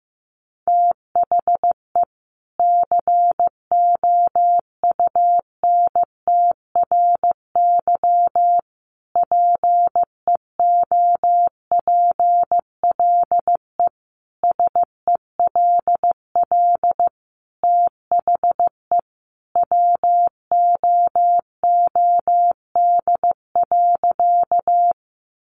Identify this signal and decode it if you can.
Morse code, heard as a keyed tone, about 15 wpm, 700 Hz. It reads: THE COUNTRY PEOPLE SELL THE WOOD.
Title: Sentences from Top 1100 Words 15wpm